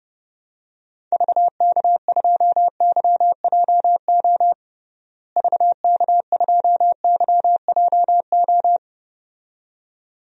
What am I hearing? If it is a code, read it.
Morse code, heard as a keyed tone, about 28 wpm, 700 Hz. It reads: VK2YJO VK2YJO
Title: DXCC-100 Call Signs - 6 Characters - Repeated 2x - 30wpm